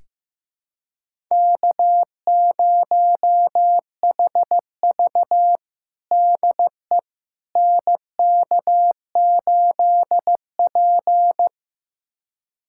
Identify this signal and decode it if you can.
Morse code, heard as a keyed tone, about 15 wpm, 700 Hz. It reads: K0HV DE NK8P
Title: USA Callsign DE Callsign - 4 Characters - 15wpm